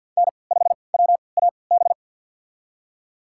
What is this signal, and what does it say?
Morse code, heard as a keyed tone, about 53 wpm, spaced as for 29 wpm, 700 Hz. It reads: N5WAB
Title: USA Call Signs - 3x Character Spacing - 50wpm